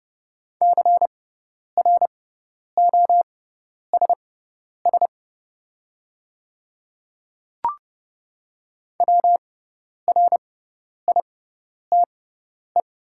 Morse code, heard as a keyed tone, about 30 wpm, spaced as for 10 wpm, 700 Hz. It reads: CROSS WRITE
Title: 6x Character Spacing - Top 300 Words - 30wpm